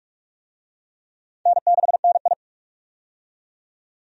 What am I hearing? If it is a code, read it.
Morse code, heard as a keyed tone, about 43 wpm, 700 Hz. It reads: N6NI
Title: USA Call Signs - 4 Characters - 45wpm